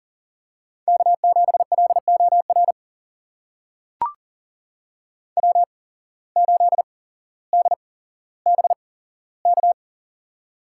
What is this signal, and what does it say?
Morse code, heard as a keyed tone, about 40 wpm, spaced as for 17 wpm, 700 Hz. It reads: K7LOR W8DBK
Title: USA Call Signs - 5 Characters - 8x Character Spacing - 40wpm